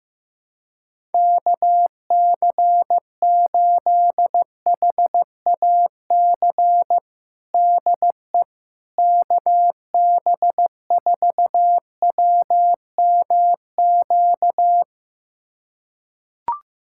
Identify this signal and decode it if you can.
Morse code, heard as a keyed tone, about 15 wpm, 700 Hz. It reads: KC8HAC DE KB4WMQ
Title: USA Callsign DE Callsign - 6 Characters - 15wpm